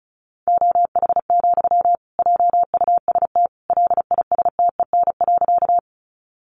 Morse code, heard as a keyed tone, about 34 wpm, 700 Hz. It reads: OH, JUST LISTEN.
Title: Sentences from Top 500 Words 35wpm